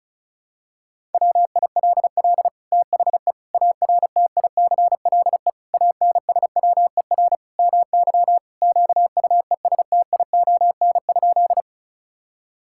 Morse code, heard as a keyed tone, about 36 wpm, 700 Hz. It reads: WILL THE ARTICLE ANSWER MY QUESTION?
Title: Sentences from Top 1000 Words - Review Part 5 - 35wpm